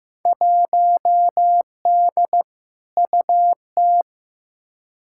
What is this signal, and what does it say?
Morse code, heard as a keyed tone, about 15 wpm, 700 Hz. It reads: 1D UT
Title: Field Day Exchanges - 15wpm